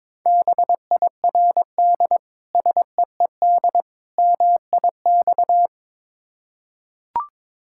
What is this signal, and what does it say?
Morse code, heard as a keyed tone, about 23 wpm, 690 Hz. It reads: BIRD SEED MIX
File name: Difficult Words with Hints - 22wpm